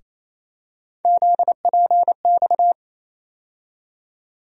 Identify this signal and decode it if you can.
Morse code, heard as a keyed tone, about 28 wpm, 700 Hz. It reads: ZPX
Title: USA Call Sign Postfixes - 3 Letters 28wpm